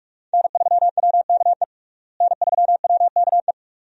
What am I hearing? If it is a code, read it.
Morse code, heard as a keyed tone, about 42 wpm, 700 Hz. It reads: N3WKE N3WKE
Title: USA Call Signs - 5 Characters - Repeated 2x - 45wpm